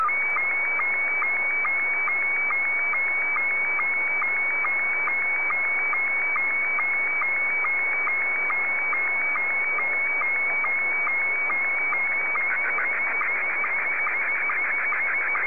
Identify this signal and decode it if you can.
SSTV (slow-scan television), Scottie 1.